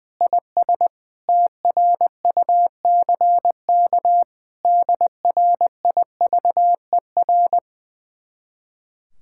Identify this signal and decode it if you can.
Morse code, heard as a keyed tone, about 21 wpm, 700 Hz. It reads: IS TRUCK DRIVER
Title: Speed-Racing Common QSO Phrases Encore - 20wpm